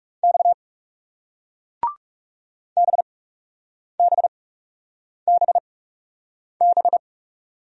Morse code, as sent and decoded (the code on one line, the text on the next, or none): -..- / -... -... -... -...
X BBBB